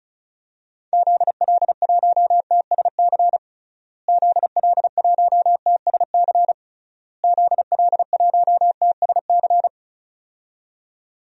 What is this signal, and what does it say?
Morse code, heard as a keyed tone, about 34 wpm, 700 Hz. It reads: ZL1TSC ZL1TSC ZL1TSC
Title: DXCC-100 Call Signs - 6 Characters - Repeated 3x - 35wpm